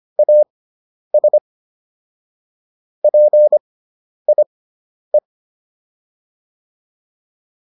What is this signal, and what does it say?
Morse code, heard as a keyed tone, about 25 wpm, spaced as for 10 wpm, 600 Hz. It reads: AS PIE